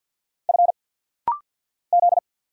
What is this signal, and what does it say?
Morse code, heard as a keyed tone, about 50 wpm, 700 Hz. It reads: F Z
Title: Single Letters - Rapid-Fire 50wpm